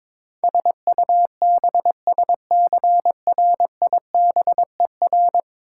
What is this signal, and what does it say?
Morse code, heard as a keyed tone, about 21 wpm, 700 Hz. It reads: SUBSCRIBER
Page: morse-code-ninja-practice-22wpm